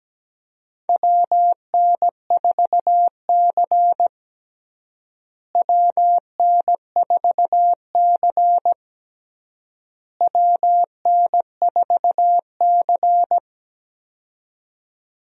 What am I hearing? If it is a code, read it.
Morse code, heard as a keyed tone, about 17 wpm, 700 Hz. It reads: WN4C WN4C WN4C
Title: USA Call Signs - 4 Characters - Repeated 3x - 17wpm